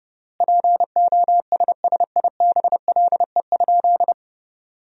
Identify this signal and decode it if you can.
Morse code, heard as a keyed tone, about 30 wpm, 700 Hz. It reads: POSSIBLE?
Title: 4x Word-Spacing - Sentences from Top 500 Words - 30wpm